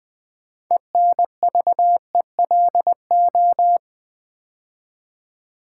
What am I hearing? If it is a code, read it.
Morse code, heard as a keyed tone, about 20 wpm, 700 Hz. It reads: ENVELO
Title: Build A Word - 2 Letters At A Time - 8-Letter Words - 20wpm